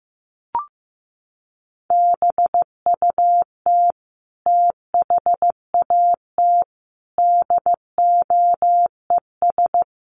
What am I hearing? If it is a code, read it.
Morse code, heard as a keyed tone, about 15 wpm, 700 Hz. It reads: BUT THAT DOES
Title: Sets of 5 Words 15wpm